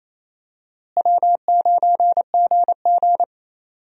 Morse code, heard as a keyed tone, about 27 wpm, 700 Hz. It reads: W9GG
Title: USA Call Signs - 4 Characters - Repeated 3x - 28wpm